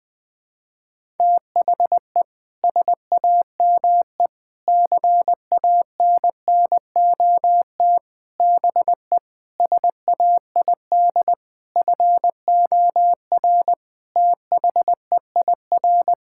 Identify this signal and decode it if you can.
Morse code, heard as a keyed tone, about 20 wpm, 700 Hz. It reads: THE SAME CANNOT BE SAID FOR THEIR
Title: Sentences with 7-Word Phrases - 20wpm